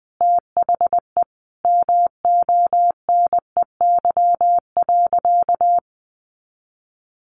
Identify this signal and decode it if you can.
Morse code, heard as a keyed tone, about 20 wpm, 700 Hz. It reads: THE MONEY.
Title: Sentences from Top 1300 Words 20wpm